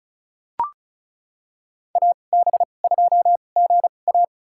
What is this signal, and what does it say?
Morse code, heard as a keyed tone, about 34 wpm, spaced as for 25 wpm, 700 Hz. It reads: AB2GA
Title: USA Call Signs - 5 Characters - 2x Character Spacing - 35wpm